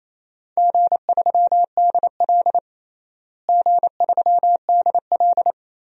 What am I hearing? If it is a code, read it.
Morse code, heard as a keyed tone, about 28 wpm, 700 Hz. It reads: G3DL G3DL
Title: DXCC-100 Call Signs - 4 Characters - Repeated 3x - 28wpm